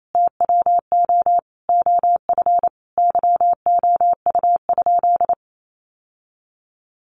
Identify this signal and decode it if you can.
Morse code, heard as a keyed tone, about 28 wpm, 700 Hz. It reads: TWO OF YOU?
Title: Sentences from Top 1400 Words 28wpm